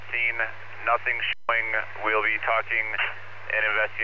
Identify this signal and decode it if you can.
Emergency Alert System (SAME) tones, heard as harsh buzzing bursts, with speech. It is heard from 0.3 to 3.2 s.